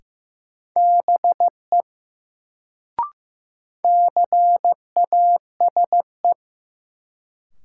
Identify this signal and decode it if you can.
Morse code, heard as a keyed tone, about 15 wpm, 700 Hz. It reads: BE CASE
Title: Sets of 2 Words - New Word Drill - 15wpm